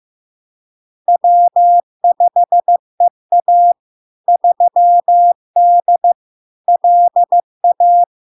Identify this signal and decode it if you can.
Morse code, heard as a keyed tone, about 15 wpm, 700 Hz. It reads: W5EA 3D LA
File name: LARK-FD-2022-Log-in-CW-15wpm-700-Hz